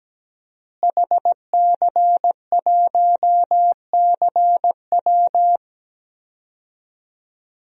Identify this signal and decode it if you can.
Morse code, heard as a keyed tone, about 17 wpm, 700 Hz. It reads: HC1CW
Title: DXCC-100 Call Signs - 5 Characters - Repeated 3x - 17wpm